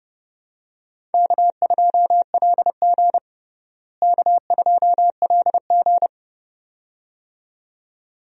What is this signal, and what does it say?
Morse code, heard as a keyed tone, about 30 wpm, 700 Hz. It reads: K2LG K2LG
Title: USA Call Signs - 4 Characters - Repeated 2x - 30wpm